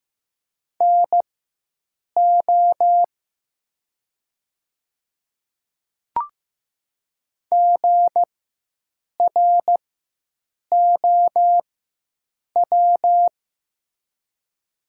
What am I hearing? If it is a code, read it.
Morse code, heard as a keyed tone, about 15 wpm, spaced as for 7 wpm, 700 Hz. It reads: NO GROW
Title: 4x Character Spacing - Top 500 Words - 15wpm